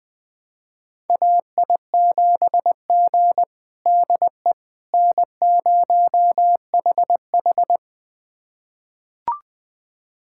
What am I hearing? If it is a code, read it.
Morse code, heard as a keyed tone, about 21 wpm, 700 Hz. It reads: AI7G DE N0HH
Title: USA Callsign DE Callsign - 4 Characters - 20wpm